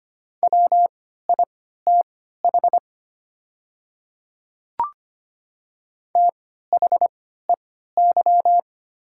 Morse code, heard as a keyed tone, about 25 wpm, spaced as for 14 wpm, 690 Hz. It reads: WITH THEY
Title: 3x Character Spacing - Top 200 Words - 25wpm